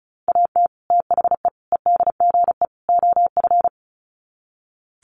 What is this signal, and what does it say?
Morse code, heard as a keyed tone, about 34 wpm, 700 Hz. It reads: AT THE EDGE OF
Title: Sets of 4 Words 35wpm